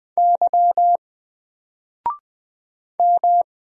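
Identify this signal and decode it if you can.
Morse code, heard as a keyed tone, about 20 wpm, 700 Hz. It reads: Y M
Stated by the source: Single letter and number Morse drill at 20 wpm, from N4LCD.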